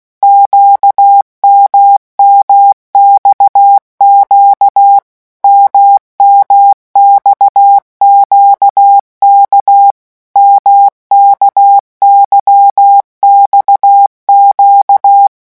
Morse code, heard as a keyed tone, about 16 wpm, 800 Hz. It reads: QMMXQ MMXQK MKYXQ
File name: morse mp3 / revision1-vitesse 15 mots